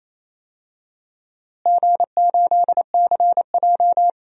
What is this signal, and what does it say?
Morse code, heard as a keyed tone, about 27 wpm, 700 Hz. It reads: G8CJ
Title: DXCC-100 Call Signs - 4 Characters - Repeated 3x - 28wpm